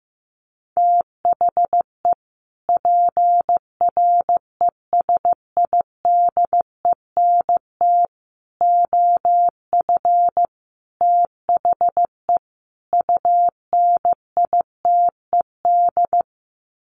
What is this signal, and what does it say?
Morse code, heard as a keyed tone, about 15 wpm, 700 Hz. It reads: THE PRESIDENT OF THE UNITED
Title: Sets of 5 Words 15wpm